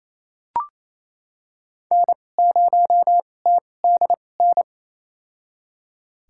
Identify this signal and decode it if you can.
Morse code, heard as a keyed tone, about 27 wpm, spaced as for 20 wpm, 700 Hz. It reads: N0TDN
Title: USA Call Signs - 2x Character Spacing - 28wpm